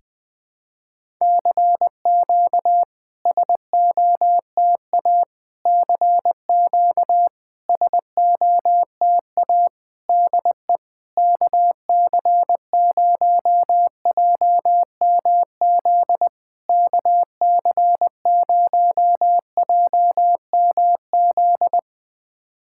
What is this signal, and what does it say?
Morse code, heard as a keyed tone, about 20 wpm, 700 Hz. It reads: CQ SOTA CQ SOTA DE KC0JMZ KC0JMZ
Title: SOTA QSO Practice 20wpm